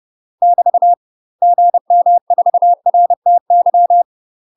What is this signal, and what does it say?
Morse code, heard as a keyed tone, about 30 wpm, 700 Hz. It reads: <BT> GM4RTY
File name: UK MASTER QSOs TRAINING / 30WPM / UK MASTER QSOs 0046